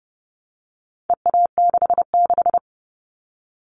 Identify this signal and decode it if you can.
Morse code, heard as a keyed tone, about 30 wpm, 700 Hz. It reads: EA66
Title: All Cut Numbers - 4-Digit 30wpm